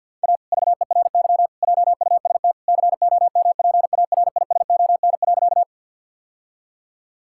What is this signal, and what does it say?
Morse code, heard as a keyed tone, about 46 wpm, 700 Hz. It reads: A VERY JUST COMPARISON.
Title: Sentences from Top 1600 Words 50wpm